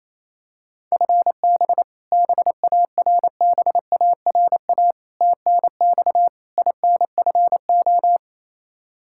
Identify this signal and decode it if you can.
Morse code, heard as a keyed tone, about 28 wpm, 700 Hz. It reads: FB BARBARA TNX INFO
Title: Practice QSO - Exchange 3 - More Info - 28wpm